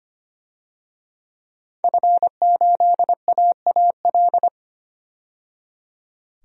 Morse code, heard as a keyed tone, about 25 wpm, 700 Hz. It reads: F8AAL
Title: DXCC-100 Call Signs - 5 Characters - 25wpm